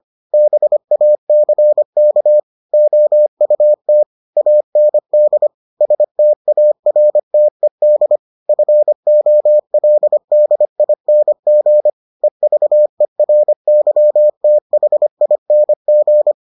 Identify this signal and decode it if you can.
Morse code, heard as a keyed tone, about 25 wpm, 600 Hz. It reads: BACK OUT AND STARTED FOLDING EVERYTHING